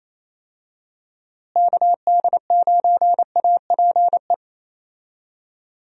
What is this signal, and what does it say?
Morse code, heard as a keyed tone, about 28 wpm, 700 Hz. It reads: KD9APE
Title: USA Call Signs - 6 Characters - Repeated 2x - 28wpm